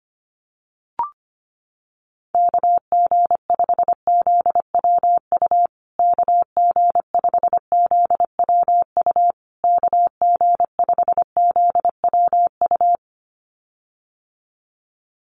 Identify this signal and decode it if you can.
Morse code, heard as a keyed tone, about 25 wpm, 700 Hz. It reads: KG5ZWU KG5ZWU KG5ZWU
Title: USA Call Signs Repeated 3x 25wpm